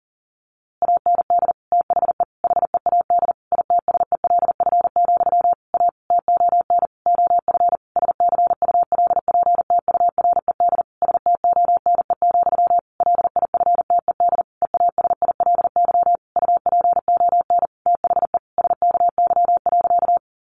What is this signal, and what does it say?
Morse code, heard as a keyed tone, about 40 wpm, 700 Hz. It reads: AND THE HEAD ITSELF, A TON OF SCULPTURED STONE, LIFTED EASILY UPON THE SKY.
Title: A Sound Of Thunder 40wpm